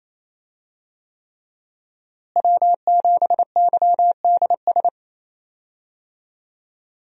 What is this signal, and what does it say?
Morse code, heard as a keyed tone, about 28 wpm, 700 Hz. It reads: W7YDS
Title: USA Call Signs - 5 Characters - Repeated 3x - 28wpm